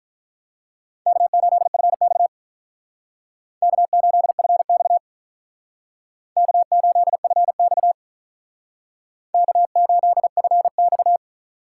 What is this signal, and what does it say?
Morse code, heard as a keyed tone, about 45 wpm, 700 Hz. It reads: K8FX K8FX K8FX K8FX